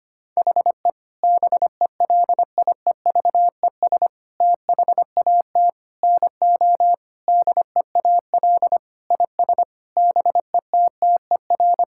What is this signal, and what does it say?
Morse code, heard as a keyed tone, about 25 wpm, 700 Hz. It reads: HE BELIEVES THAT NO DEAL IS BETTER
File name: practice / 25wpm / Sentences with 9-Word Phrases - 25wpm